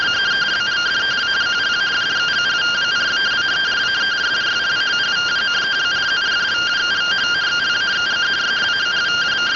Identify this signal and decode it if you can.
Two-tone teleprinter signal, about 170 Hz shift.